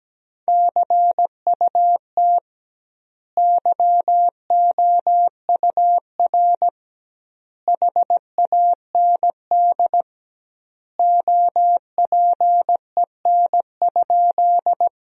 Morse code, heard as a keyed tone, about 17 wpm, 700 Hz. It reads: CUT YOUR HAND OPEN?
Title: 2x Word-Spacing - Sentences from Top 300 Words - 17wpm